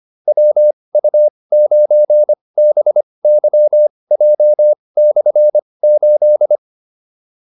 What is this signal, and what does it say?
Morse code, heard as a keyed tone, about 25 wpm, spaced as for 20 wpm, 600 Hz. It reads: WU9BYJ/8